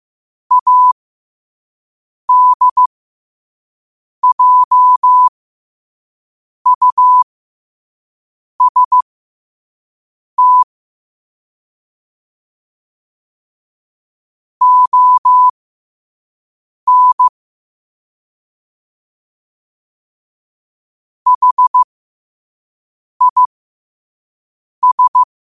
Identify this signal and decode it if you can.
Morse code, heard as a keyed tone, about 15 wpm, spaced as for 5 wpm, 1000 Hz. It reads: ADJUST ON HIS
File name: qotdfiles / 5wpm